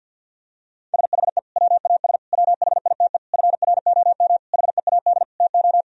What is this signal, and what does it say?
Morse code, heard as a keyed tone, about 50 wpm, 700 Hz. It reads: SHE WAS WHITE FROM HEAD TO
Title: Sentences from Top 1000 Words - Review Part 3 - 50wpm